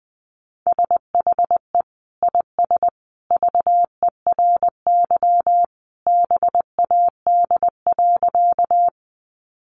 Morse code, heard as a keyed tone, about 20 wpm, 700 Hz. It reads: SHE IS VERY BAD.
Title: Sentences from Top 700 Words 20wpm